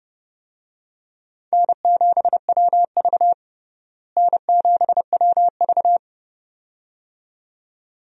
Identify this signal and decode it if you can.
Morse code, heard as a keyed tone, about 30 wpm, 700 Hz. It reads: N7WV N7WV
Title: USA Call Signs - 4 Characters - Repeated 2x - 30wpm